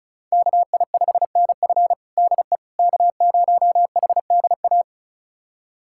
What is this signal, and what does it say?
Morse code, heard as a keyed tone, about 36 wpm, 700 Hz. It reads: KI5NF DE K0HDA
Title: USA Callsign DE Callsign - 5 Characters - 35wpm